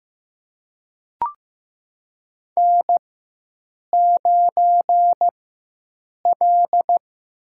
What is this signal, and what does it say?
Morse code, heard as a keyed tone, about 15 wpm, spaced as for 7 wpm, 700 Hz. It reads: N9L
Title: USA Call Signs - 4x Character Spacing - 15wpm